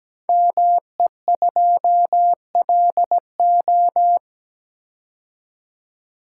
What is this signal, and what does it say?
Morse code, heard as a keyed tone, about 17 wpm, 700 Hz. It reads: ME2LO
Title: Code Groups - 5 Letter:Numbers - 17wpm